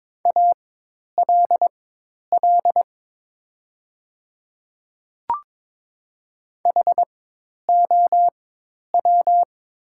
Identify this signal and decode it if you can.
Morse code, heard as a keyed tone, about 22 wpm, spaced as for 10 wpm, 700 Hz. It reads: ALL HOW
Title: 4x Character Spacing - Top 100 Words - 22wpm